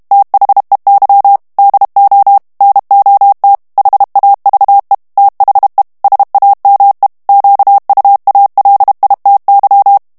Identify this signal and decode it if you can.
Morse code, heard as a keyed tone, about 32 wpm, 800 Hz. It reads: THEY DO NOT HAVE THE SAME QUALITY
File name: morse mp3 / ANGLAIS texte clair 1 vitesse 30 mots minute